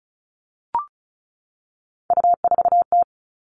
Morse code, heard as a keyed tone, about 34 wpm, 700 Hz. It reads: U4T
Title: All Cut Numbers - 3-Digit 35wpm